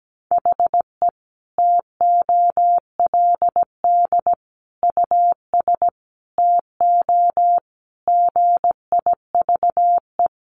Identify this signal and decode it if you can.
Morse code, heard as a keyed tone, about 17 wpm, 700 Hz. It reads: HE TOLD US TO GIVE
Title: Sentences from Top 1000 Words 17wpm